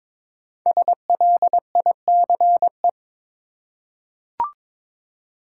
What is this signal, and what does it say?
Morse code, heard as a keyed tone, about 22 wpm, 700 Hz. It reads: SLICE
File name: 5 Letter Words - First 3 Letters Given - 22wpm